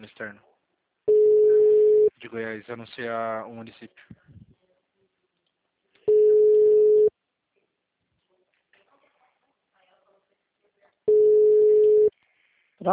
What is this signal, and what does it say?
Ringback tone (Europe, 425 Hz), with speech.